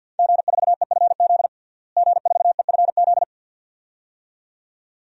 Morse code, heard as a keyed tone, about 46 wpm, 700 Hz. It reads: G4EFZ G4EFZ
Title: DXCC-100 Call Signs - 5 Characters - Repeated 2x - 50wpm